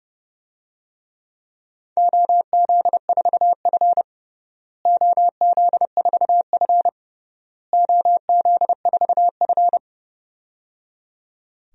Morse code, heard as a keyed tone, about 30 wpm, 700 Hz. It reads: OZ4F OZ4F OZ4F
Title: DXCC-100 Call Signs - 4 Characters - Repeated 3x - 30wpm